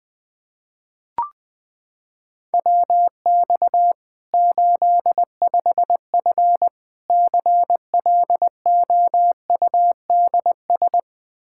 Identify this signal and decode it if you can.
Morse code, heard as a keyed tone, about 20 wpm, 700 Hz. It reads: WX 85F CLOUDS
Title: Practice QSO - Exchange 2 - Info - 20wpm